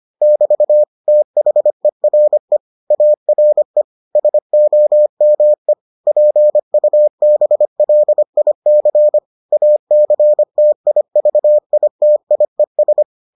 Morse code, heard as a keyed tone, about 25 wpm, 600 Hz. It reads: <BT> THERE ARE SOME PUBLIC ACTIVITIES